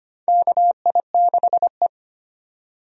Morse code, heard as a keyed tone, about 25 wpm, 700 Hz. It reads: KI6E
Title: USA Call Signs - 4 Characters - 25wpm